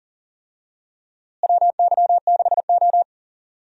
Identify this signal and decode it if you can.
Morse code, heard as a keyed tone, about 40 wpm, 700 Hz. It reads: WY6O
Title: USA Call Signs - 4 Characters - Repeated 2x - 40wpm